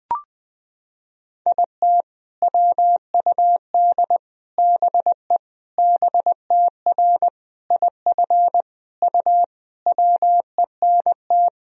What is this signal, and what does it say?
Morse code, heard as a keyed tone, about 20 wpm, 700 Hz. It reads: IT WUD BE BTR IF U WENT
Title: Sentences from Top 400 Words and Abbreviations 20wpm